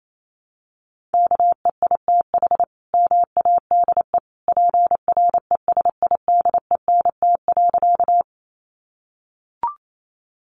Morse code, heard as a keyed tone, about 28 wpm, 700 Hz. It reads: KEITH MADE PRESIDENT.
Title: A Sound Of Thunder 28wpm